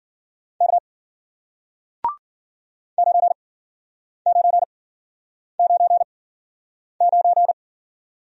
Morse code, heard as a keyed tone, about 56 wpm, spaced as for 9 wpm, 700 Hz. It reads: K 9999